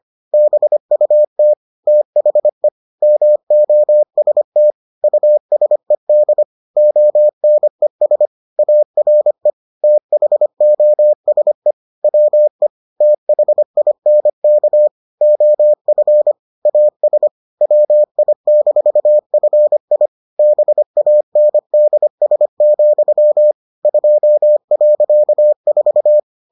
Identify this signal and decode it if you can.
Morse code, heard as a keyed tone, about 25 wpm, 600 Hz. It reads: BUT THE MOST USED ONES ARE THOSE WE THINK OF AS WI-FI BANDS, 2.4